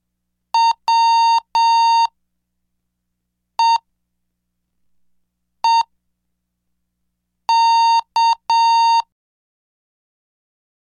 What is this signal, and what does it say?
Morse code, heard as a keyed tone, about 7 wpm, spaced as for 4 wpm, 920 Hz. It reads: WEEK
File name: cursos / CDestudo / 11 Track 11